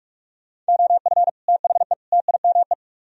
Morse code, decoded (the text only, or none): OF THE TIME